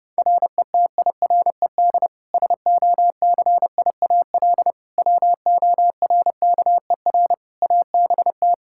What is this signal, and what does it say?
Morse code, heard as a keyed tone, about 32 wpm, 700 Hz. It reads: RETIRED SOCIAL WORKER ABT
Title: K1USN SST QSO Practice 30wpm